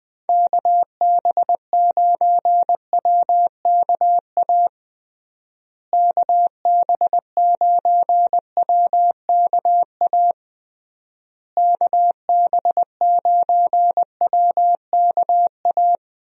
Morse code, heard as a keyed tone, about 20 wpm, 700 Hz. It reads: KB9WKA KB9WKA KB9WKA
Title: USA Call Signs - 6 Characters - Repeated 3x - 20wpm